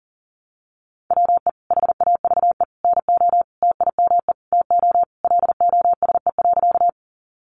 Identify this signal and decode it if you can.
Morse code, heard as a keyed tone, about 40 wpm, 700 Hz. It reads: WE HAVE NO TIME TO LOSE.
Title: Sentences from Top 1100 Words 40wpm